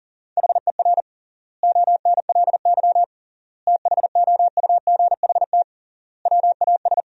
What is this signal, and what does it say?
Morse code, heard as a keyed tone, about 40 wpm, 700 Hz. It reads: HER ONLY THOUGHT WAS
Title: 3x Word-Spacing - Sentences from Top 400 Words - 40wpm